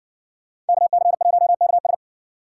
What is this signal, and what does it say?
Morse code, heard as a keyed tone, about 55 wpm, 700 Hz. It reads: KC1KS